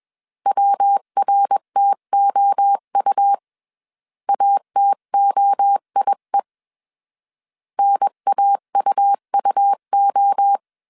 Morse code, heard as a keyed tone, about 21 wpm, 800 Hz. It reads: WRTOU ATOIE NAUUO